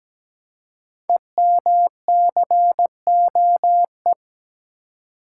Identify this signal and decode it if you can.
Morse code, heard as a keyed tone, about 17 wpm, 700 Hz. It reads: EMCOE
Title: Code Groups - 5 Letter:Numbers - 17wpm